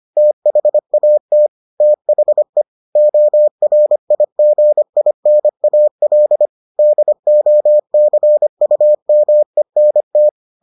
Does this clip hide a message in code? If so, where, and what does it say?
Yes: Morse code from 0.2 to 10.3 s, heard as a keyed tone, about 25 wpm, 600 Hz. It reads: THAT THE ORIGINAL DOCUMENT